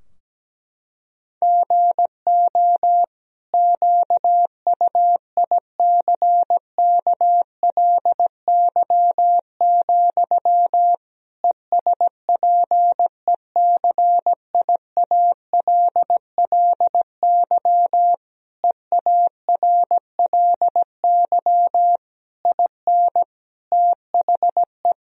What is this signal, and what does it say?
Morse code, heard as a keyed tone, about 17 wpm, 700 Hz. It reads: GO QUICKLY, ESPECIALLY EARLY IN THE
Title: Sentences from Top 1000 Words - Review Part 4 - 17wpm